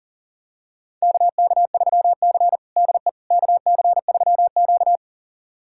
Morse code, heard as a keyed tone, about 40 wpm, 700 Hz. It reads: KK3C DE KC3Q
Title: USA Callsign DE Callsign - 4 Characters - 40wpm